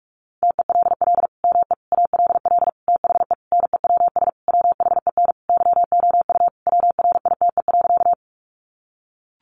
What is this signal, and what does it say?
Morse code, heard as a keyed tone, about 44 wpm, 700 Hz. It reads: TELL ME ALL THE NEWS WHEN YOU WRITE.
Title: Sentences from Top 800 Words 45wpm